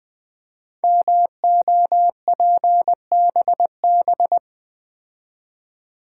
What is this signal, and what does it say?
Morse code, heard as a keyed tone, about 20 wpm, 700 Hz. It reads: MOPBB
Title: Code Groups - 5 Letters - 20wpm